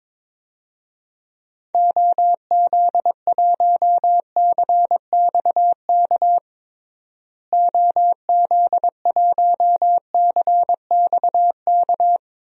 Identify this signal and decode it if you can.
Morse code, heard as a keyed tone, about 22 wpm, 700 Hz. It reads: OZ1CXK OZ1CXK
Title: DXCC-100 Call Signs - 6 Characters - Repeated 3x - 22wpm